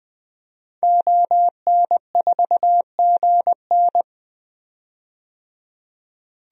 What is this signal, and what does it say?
Morse code, heard as a keyed tone, about 20 wpm, 700 Hz. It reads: ON4GN
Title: DXCC-100 Call Signs - 5 Characters - 20wpm